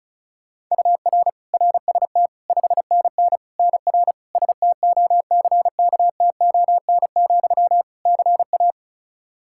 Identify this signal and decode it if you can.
Morse code, heard as a keyed tone, about 34 wpm, 700 Hz. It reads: UR RST 5NN NR STOCKTON, CA
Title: Practice QSO - Exchange 1 - Beginning - 35wpm